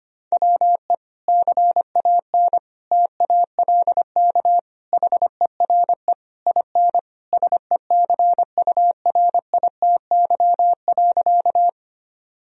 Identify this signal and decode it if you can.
Morse code, heard as a keyed tone, about 25 wpm, 700 Hz. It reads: WE CAN TALK HERE IN SECURITY.
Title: Sentences from Top 1000 Words - Review Part 4 - 25wpm